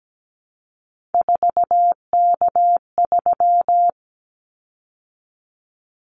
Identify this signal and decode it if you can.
Morse code, heard as a keyed tone, about 17 wpm, 700 Hz. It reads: 4K3
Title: Callsign Prefixes for Top 100 DXCC - 17wpm